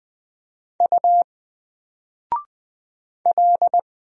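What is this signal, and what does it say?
Morse code, heard as a keyed tone, about 20 wpm, 700 Hz. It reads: U L
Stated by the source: Single letter and number Morse drill at 20 wpm, from N4LCD.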